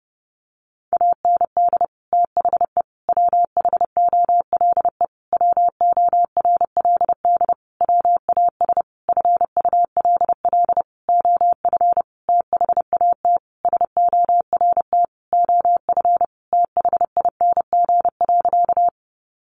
Morse code, heard as a keyed tone, about 30 wpm, 700 Hz. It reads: AND THE WHOLE WORLD WAS FULL OF THAT SORT OF THING.
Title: Sentences from Top 1300 Words 30wpm